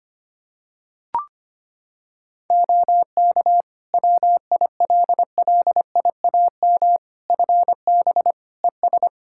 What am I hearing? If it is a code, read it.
Morse code, heard as a keyed tone, about 25 wpm, 700 Hz. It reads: OK WILLIAM FB ES
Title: Practice QSO - Exchange 2 - Info - 25wpm